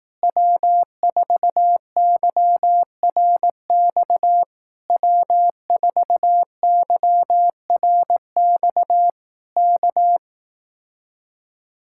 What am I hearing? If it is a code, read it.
Morse code, heard as a keyed tone, about 18 wpm, 700 Hz. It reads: W4YRX W4YRX K
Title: Speed-Racing Common QSO Phrases Encore - 15wpm